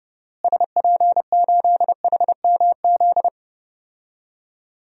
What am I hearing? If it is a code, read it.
Morse code, heard as a keyed tone, about 32 wpm, 700 Hz. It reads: SP8HMZ
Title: DXCC-100 Call Signs - 6 Characters - Repeated 3x - 30wpm